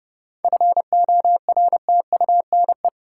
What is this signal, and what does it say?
Morse code, heard as a keyed tone, about 30 wpm, 700 Hz. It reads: FORTUNE
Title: Seven Letter Words - 30wpm